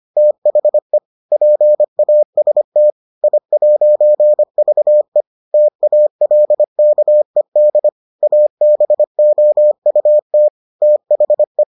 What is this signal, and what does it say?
Morse code, heard as a keyed tone, about 25 wpm, 600 Hz. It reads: THE PAST I'VE TALKED ABOUT THE